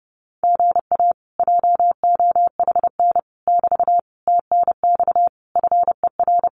Morse code, heard as a keyed tone, about 30 wpm, 700 Hz. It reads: GA JOHN <BT> TNX FER
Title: Common QSO Phrases Encore 30wpm